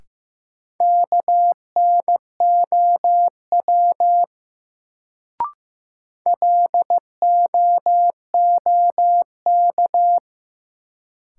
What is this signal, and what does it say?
Morse code, heard as a keyed tone, about 15 wpm, 700 Hz. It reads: KNOW LOOK
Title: Sentences from Top 100 Words - New Word Drill - 15wpm